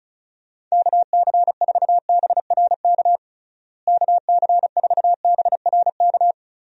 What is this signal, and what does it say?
Morse code, heard as a keyed tone, about 36 wpm, 700 Hz. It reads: KC4BRK KC4BRK
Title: USA Call Signs - 6 Characters - Repeated 2x - 35wpm